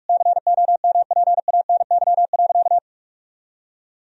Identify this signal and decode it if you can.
Morse code, heard as a keyed tone, about 42 wpm, 700 Hz. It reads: COMPANY.
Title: Sentences from Top 1000 Words - Review Part 3 - 45wpm